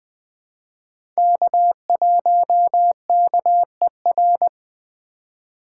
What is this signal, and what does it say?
Morse code, heard as a keyed tone, about 20 wpm, 700 Hz. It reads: K1KER
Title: USA Call Signs - 5 Characters - 4x Character Spacing - 20wpm